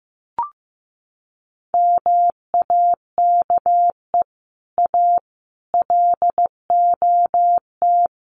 Morse code, heard as a keyed tone, about 15 wpm, 700 Hz. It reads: MAKE A LOT
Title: Sets of 5 Words Encore2 15wpm